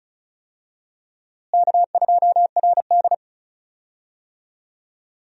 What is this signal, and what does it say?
Morse code, heard as a keyed tone, about 36 wpm, 700 Hz. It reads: K2RD
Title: USA Call Signs - 4 Characters - Repeated 2x - 35wpm